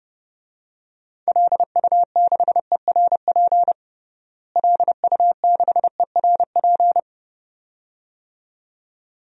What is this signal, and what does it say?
Morse code, heard as a keyed tone, about 32 wpm, 700 Hz. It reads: LU6ERP LU6ERP
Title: DXCC-100 Call Signs - 6 Characters - Repeated 2x - 30wpm